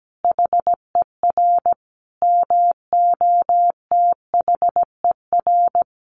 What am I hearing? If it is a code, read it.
Morse code, heard as a keyed tone, about 17 wpm, 700 Hz. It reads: HER MOTHER
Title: Sentences from Top 200 Words 17wpm